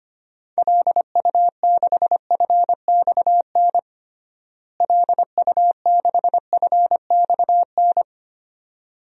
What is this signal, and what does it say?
Morse code, heard as a keyed tone, about 25 wpm, 700 Hz. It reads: LU6FXN LU6FXN
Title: DXCC-100 Call Signs - 6 Characters - Repeated 2x - 25wpm